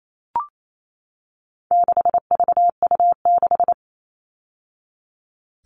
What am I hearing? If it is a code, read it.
Morse code, heard as a keyed tone, about 28 wpm, 700 Hz. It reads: 6VU6
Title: All Cut Numbers - 4-Digit 28wpm